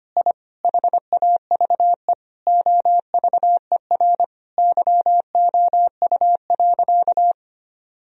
Morse code, heard as a keyed tone, about 25 wpm, 700 Hz. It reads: I HAVE OVER YOU.
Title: Sentences from Top 1000 Words - Review Part 1 - 25wpm